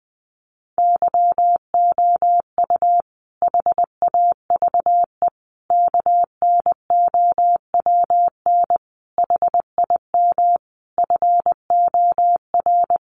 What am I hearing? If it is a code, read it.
Morse code, heard as a keyed tone, about 20 wpm, 700 Hz. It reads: YOU HAVE KNOWN HIM FOR
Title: Sentences from Top 600 Words 20wpm